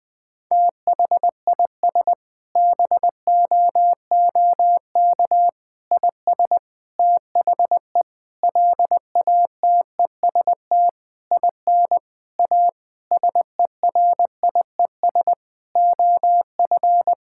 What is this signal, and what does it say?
Morse code, heard as a keyed tone, about 21 wpm, 700 Hz. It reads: THIS BOOK IS THE LATEST IN A SERIES OF
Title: Sentences with 7-Word Phrases - 20wpm